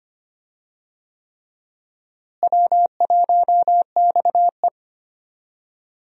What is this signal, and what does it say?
Morse code, heard as a keyed tone, about 25 wpm, 700 Hz. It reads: W1XE
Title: USA Call Signs - 4 Characters - Repeated 3x - 25wpm